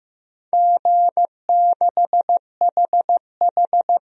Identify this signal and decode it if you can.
Morse code, heard as a keyed tone, about 15 wpm, 700 Hz. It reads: G6HH
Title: DXCC-100 Call Signs - 4 Characters - 15wpm